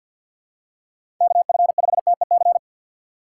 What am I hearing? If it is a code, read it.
Morse code, heard as a keyed tone, about 50 wpm, 700 Hz. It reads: KF5TEC
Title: USA Call Signs - 6 Characters - 6x Character Spacing - 50wpm